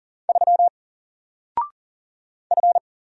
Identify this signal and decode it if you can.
Morse code, heard as a keyed tone, about 40 wpm, 700 Hz. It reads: 3 F
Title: Single Character (Letter:Number:Punctuation:Prosign) Alt 40wpm